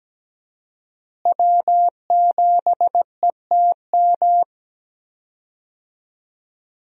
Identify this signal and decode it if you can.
Morse code, heard as a keyed tone, about 17 wpm, 700 Hz. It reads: W7ETM
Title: USA Call Signs - 5 Characters - Repeated 2x - 17wpm